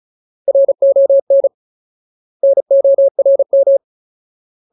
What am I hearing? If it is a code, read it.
Morse code, heard as a keyed tone, about 35 wpm, 550 Hz. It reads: RON NORM
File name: Ultimate Word List / 35WPM Ultimate